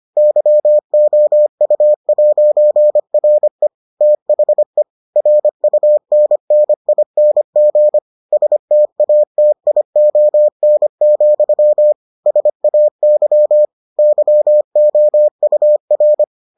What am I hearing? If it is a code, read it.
Morse code, heard as a keyed tone, about 25 wpm, 600 Hz. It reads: YOU'RE THE RUNNING STATION, SAY YOUR